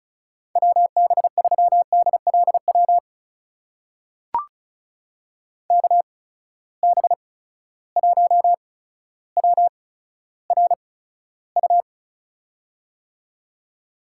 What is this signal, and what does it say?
Morse code, heard as a keyed tone, about 36 wpm, spaced as for 15 wpm, 700 Hz. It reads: WB3DLW KB1WRU
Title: USA Call Signs - 6 Characters - 8x Character Spacing - 35wpm